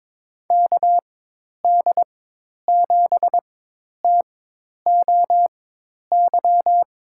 Morse code, heard as a keyed tone, about 22 wpm, spaced as for 10 wpm, 700 Hz. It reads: KD7TOY
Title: USA Call Signs - 4x Character Spacing - 22wpm